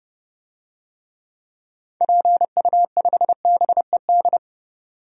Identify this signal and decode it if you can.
Morse code, heard as a keyed tone, about 29 wpm, 700 Hz. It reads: PU5BED
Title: DXCC-100 Call Signs - 6 Characters - 30wpm